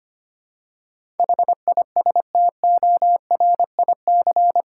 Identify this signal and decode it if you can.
Morse code, heard as a keyed tone, about 25 wpm, 700 Hz. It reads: HISTORIC
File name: Eight Letter Words - 25wpm